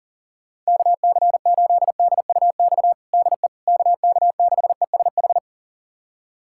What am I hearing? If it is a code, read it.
Morse code, heard as a keyed tone, about 40 wpm, 700 Hz. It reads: KC8DUX DE KK6ESH
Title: USA Callsign DE Callsign 40wpm